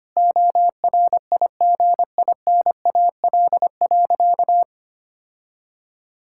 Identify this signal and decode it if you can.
Morse code, heard as a keyed tone, about 25 wpm, 700 Hz. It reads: ORIGINAL.
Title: Sentences from Top 1000 Words - Review Part 6 - 25wpm